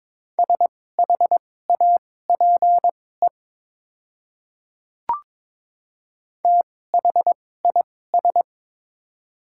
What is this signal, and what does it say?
Morse code, heard as a keyed tone, about 22 wpm, spaced as for 16 wpm, 690 Hz. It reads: SHAPE THIS
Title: 2x Character Spacing - Top 500 Words - 22wpm